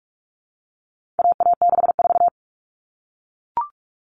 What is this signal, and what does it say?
Morse code, heard as a keyed tone, about 44 wpm, 700 Hz. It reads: AA64
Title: All Cut Numbers - 4-Digit 45wpm